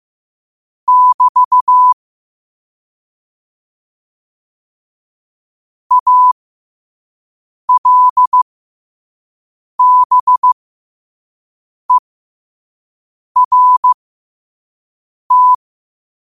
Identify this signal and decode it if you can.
Morse code, heard as a keyed tone, about 15 wpm, spaced as for 5 wpm, 1000 Hz. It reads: <BT> ALBERT
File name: qotdfiles / 5wpm